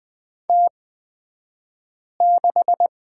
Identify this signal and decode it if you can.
Morse code, heard as a keyed tone, about 20 wpm, spaced as for 5 wpm, 700 Hz. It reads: T6
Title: 249 Lesson 40d - BK - Two Characters 20wpm